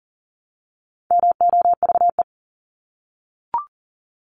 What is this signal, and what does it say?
Morse code, heard as a keyed tone, about 40 wpm, 700 Hz. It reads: MOVE
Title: ING Suffix - Top 100 Words with Base Words 40wpm